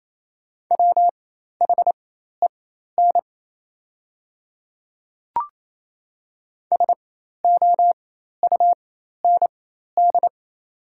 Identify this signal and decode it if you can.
Morse code, heard as a keyed tone, about 28 wpm, spaced as for 13 wpm, 700 Hz. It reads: WHEN SOUND
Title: 4x Character Spacing - Top 100 Words - 28wpm